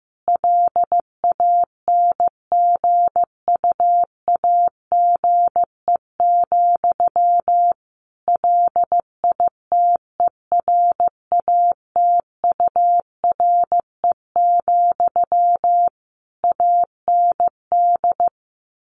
Morse code, heard as a keyed tone, about 15 wpm, 700 Hz. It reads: LANGUAGE, LITERATURE, AND
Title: Sentences from Top 600 Words 15wpm